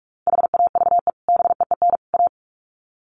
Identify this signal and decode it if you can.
Morse code, heard as a keyed tone, about 44 wpm, 700 Hz. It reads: HAVE BEEN A
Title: Sets of 3 Words 45wpm